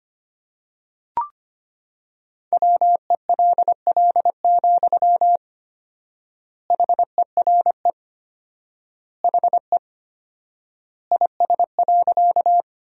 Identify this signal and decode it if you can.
Morse code, heard as a keyed tone, about 25 wpm, 700 Hz. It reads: WELL, HERE HE IS.
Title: 4x Word-Spacing - Sentences from Top 200 Words - 25wpm